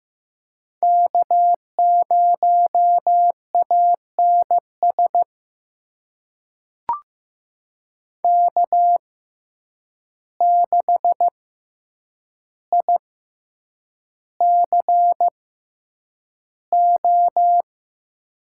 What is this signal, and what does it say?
Morse code, heard as a keyed tone, about 15 wpm, spaced as for 8 wpm, 700 Hz. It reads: K0ANS K6ICO
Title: USA Call Signs - 5 Characters - 6x Character Spacing - 15wpm